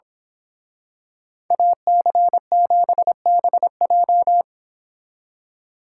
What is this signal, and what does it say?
Morse code, heard as a keyed tone, about 25 wpm, 700 Hz. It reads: AC7BJ
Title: Speed Racing - USA Call Signs 22wpm